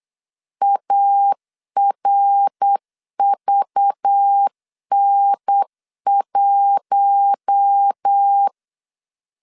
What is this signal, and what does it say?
Morse code, heard as a keyed tone, about 8 wpm, 800 Hz. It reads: ARVN1
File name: CW / DET / 1 6 1500C S B 8wpm 5142